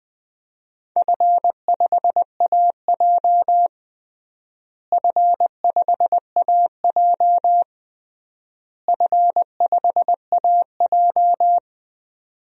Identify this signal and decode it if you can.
Morse code, heard as a keyed tone, about 20 wpm, 700 Hz. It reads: F5AJ F5AJ F5AJ
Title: DXCC-100 Call Signs - 4 Characters - Repeated 3x - 20wpm